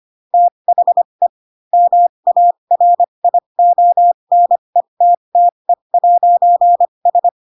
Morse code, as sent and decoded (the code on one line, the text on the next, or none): - .... / -- .- .-. .. --- -. - - . .----. ...
THE MARIONETTE'S